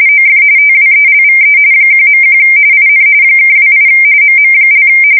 RTTY, about 45.45 baud. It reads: OF TWELVE 1.1 IN (28 MM)
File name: fsk170 / USS-Massachusetts